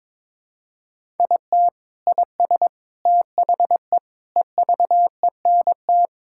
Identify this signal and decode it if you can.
Morse code, heard as a keyed tone, about 22 wpm, 700 Hz. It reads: IT IS THE EVENT
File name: practice / 22wpm / Sentences with 3-Word Phrases - 22wpm